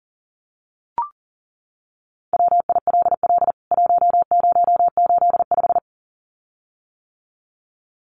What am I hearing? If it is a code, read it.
Morse code, heard as a keyed tone, about 37 wpm, 700 Hz. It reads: WILL 1085
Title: CWT Exchange 40wpm